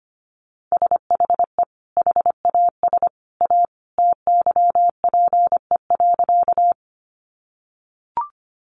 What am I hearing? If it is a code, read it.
Morse code, heard as a keyed tone, about 25 wpm, 700 Hz. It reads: SHE HAS A TYPE.
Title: Sentences from Top 600 Words 25wpm